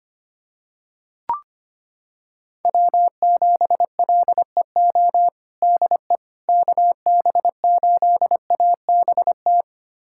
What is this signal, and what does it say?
Morse code, heard as a keyed tone, about 25 wpm, 700 Hz. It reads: W7LEO DE KB8ABT
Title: USA Callsign DE Callsign 25wpm